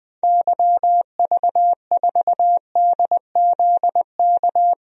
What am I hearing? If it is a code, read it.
Morse code, heard as a keyed tone, about 20 wpm, 700 Hz. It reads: YV4DZK
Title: DXCC-100 Call Signs - 6 Characters - 20wpm